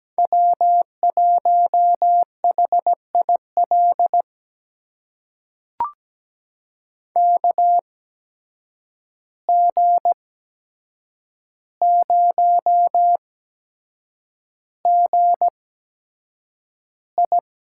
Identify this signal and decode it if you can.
Morse code, heard as a keyed tone, about 17 wpm, spaced as for 7 wpm, 700 Hz. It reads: W1HIL KG0GI
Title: USA Call Signs - 5 Characters - 8x Character Spacing - 17wpm